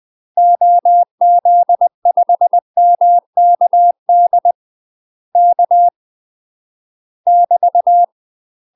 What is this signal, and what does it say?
Morse code, heard as a keyed tone, about 20 wpm, 700 Hz. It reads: OZ5MKD K <BT>
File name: UK MASTER QSOs TRAINING/20WPM